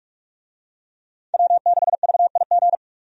Morse code, heard as a keyed tone, about 46 wpm, 700 Hz. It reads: W6VIG
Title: USA Call Signs - 5 Characters - 45wpm